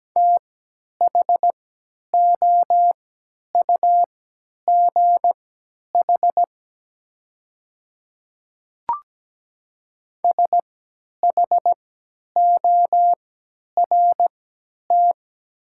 Morse code, heard as a keyed tone, about 17 wpm, spaced as for 10 wpm, 700 Hz. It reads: THOUGH SHORT
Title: 3x Character Spacing - Top 500 Words - 17wpm